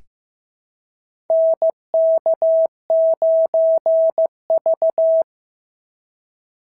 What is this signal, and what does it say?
Morse code, heard as a keyed tone, about 15 wpm, 650 Hz. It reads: NK9V
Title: SOTA QSO Practice 15wpm